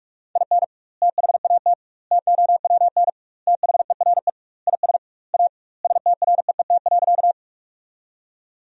Morse code, heard as a keyed tone, about 46 wpm, 700 Hz. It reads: IN THAT TOWN THERE IS A STREET.
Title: 2x Word-Spacing - Sentences from Top 500 Words - 45wpm